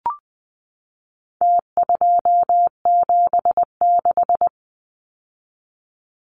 Morse code, heard as a keyed tone, about 20 wpm, 700 Hz. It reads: T276
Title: Numbers - Contest Style - 4-Digit 20wpm